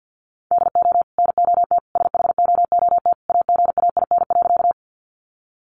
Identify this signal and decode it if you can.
Morse code, heard as a keyed tone, about 50 wpm, 700 Hz. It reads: DO NOT SHOOT AGAIN.
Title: Sentences from Top 1200 Words 50wpm